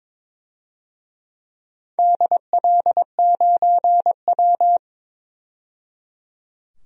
Morse code, heard as a keyed tone, about 22 wpm, 700 Hz. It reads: DL9W
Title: DXCC-100 Call Signs - 4 Characters - 22wpm